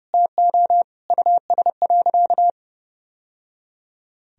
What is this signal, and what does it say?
Morse code, heard as a keyed tone, about 32 wpm, 700 Hz. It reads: TO US.
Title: Sentences from Top 1000 Words - Review Part 1 - 30wpm